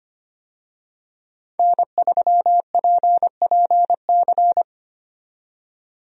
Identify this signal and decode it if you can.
Morse code, heard as a keyed tone, about 25 wpm, 700 Hz. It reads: N3PPC
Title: USA Call Signs - 5 Characters - 25wpm